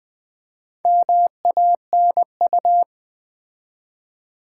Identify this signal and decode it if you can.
Morse code, heard as a keyed tone, about 20 wpm, 700 Hz. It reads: MANU
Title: Build A Word - 4 Letters At A Time - 11-Letter Words - 20wpm